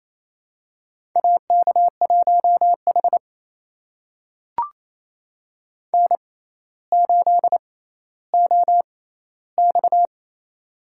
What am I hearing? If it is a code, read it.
Morse code, heard as a keyed tone, about 28 wpm, spaced as for 14 wpm, 700 Hz. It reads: AK1H N8OX
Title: USA Call Signs - 4 Characters - 6x Character Spacing - 28wpm